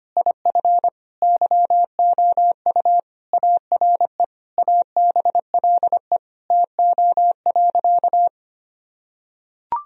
Morse code, heard as a keyed tone, about 25 wpm, 700 Hz. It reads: IF YOU ARE ABLE TO.
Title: Sentences from Top 1000 Words - Review Part 2 - 25wpm